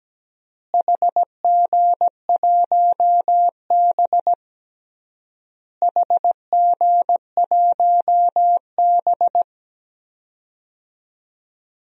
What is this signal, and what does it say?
Morse code, heard as a keyed tone, about 17 wpm, 700 Hz. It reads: HG1B HG1B
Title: DXCC-100 Call Signs - 4 Characters - Repeated 3x - 17wpm